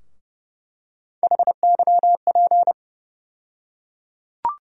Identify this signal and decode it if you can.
Morse code, heard as a keyed tone, about 30 wpm, 700 Hz. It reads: HYP
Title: USA Call Sign Postfixes - 3 Letters 30wpm